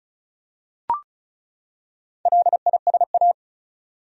Morse code, heard as a keyed tone, about 35 wpm, 700 Hz. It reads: LISA
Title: USA Common Names - Max 4 Letters - 35wpm